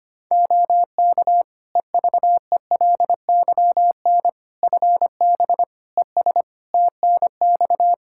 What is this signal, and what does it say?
Morse code, heard as a keyed tone, about 25 wpm, 700 Hz. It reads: OK EVELYN FB ES TNX
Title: Practice QSO - Exchange 2 - Info - 25wpm